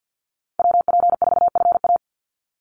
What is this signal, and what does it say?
Morse code, heard as a keyed tone, about 48 wpm, 700 Hz. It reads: WP4FA
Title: USA Call Signs 50wpm